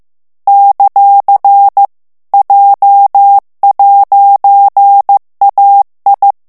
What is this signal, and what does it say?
Morse code, heard as a keyed tone, about 15 wpm, 800 Hz. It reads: ; J'AI
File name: morse mp3 / FRANCAIS TEXTE CLAIR 2 VITESSE 14 MOTS MINUTES